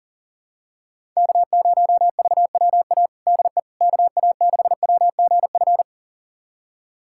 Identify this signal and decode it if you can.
Morse code, heard as a keyed tone, about 40 wpm, 700 Hz. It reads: K0VWA DE KA6WGF
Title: USA Callsign DE Callsign 40wpm